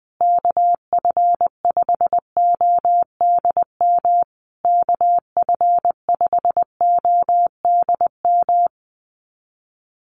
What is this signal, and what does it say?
Morse code, heard as a keyed tone, about 20 wpm, 700 Hz. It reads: KF5ODM KF5ODM
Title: USA Call Signs Repeated 3x 20wpm